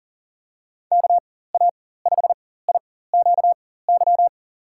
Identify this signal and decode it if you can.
Morse code, heard as a keyed tone, about 40 wpm, spaced as for 19 wpm, 700 Hz. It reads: KA5IQY
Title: USA Call Signs - 6 Characters - 4x Character Spacing - 40wpm